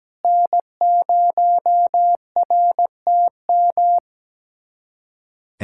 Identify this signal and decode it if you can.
Morse code, heard as a keyed tone, about 17 wpm, 700 Hz. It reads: N0RTM
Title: USA Callsign DE Callsign - 5 Characters - 17wpm